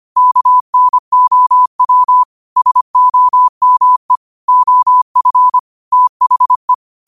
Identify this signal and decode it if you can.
Morse code, heard as a keyed tone, about 25 wpm, 1000 Hz. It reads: KNOW SOME OF THE
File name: qotdfiles / 25wpm